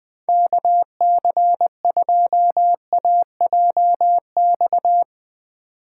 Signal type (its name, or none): Morse code